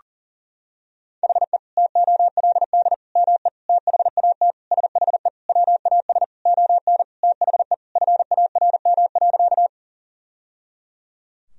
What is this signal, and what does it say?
Morse code, heard as a keyed tone, about 37 wpm, 700 Hz. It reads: HE TOLD ME THAT SHE WAS ON THE FARM.
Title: Sentences from Top 1000 Words - Review Part 2 - 40wpm